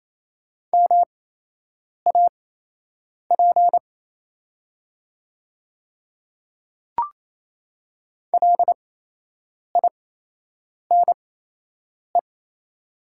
Morse code, heard as a keyed tone, about 28 wpm, spaced as for 8 wpm, 700 Hz. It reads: MAP LINE
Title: 8x Character Spacing - Top 500 Words - 28wpm